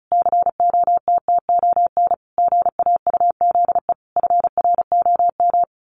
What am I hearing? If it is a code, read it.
Morse code, heard as a keyed tone, about 34 wpm, 700 Hz. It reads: COTTON GAUZE FROM
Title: A Sound Of Thunder 35wpm